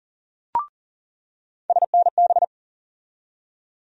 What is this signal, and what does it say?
Morse code, heard as a keyed tone, about 40 wpm, 700 Hz. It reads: SNB
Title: USA Call Sign Postfixes - 3 Letters 40wpm